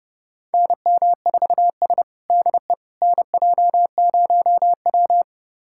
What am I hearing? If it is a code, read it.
Morse code, heard as a keyed tone, about 30 wpm, 700 Hz. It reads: NM4S DE NJ0W
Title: USA Callsign DE Callsign - 4 Characters - 30wpm